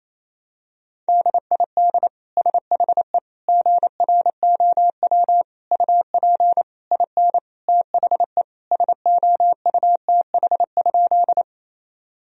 Morse code, heard as a keyed tone, about 27 wpm, 700 Hz. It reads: DID SHE GROW UP IN THE SOUTH?
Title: Sentences from Top 1000 Words - Review Part 2 - 28wpm